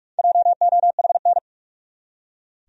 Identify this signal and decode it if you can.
Morse code, heard as a keyed tone, about 46 wpm, 700 Hz. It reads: JOHN
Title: USA Common Names - 45wpm